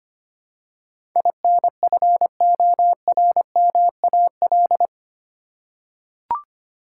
Morse code, heard as a keyed tone, about 25 wpm, 700 Hz. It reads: INFORMAL
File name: 8 Letter Words - First 3 Letters Given - 25wpm